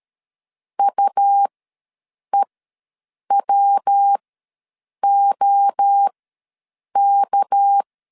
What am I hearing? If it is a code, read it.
Morse code, heard as a keyed tone, about 13 wpm, spaced as for 7 wpm, 800 Hz. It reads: UEWOK
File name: LEZIONI / 6 PKX0 / 6 PK ws 12wpm 3116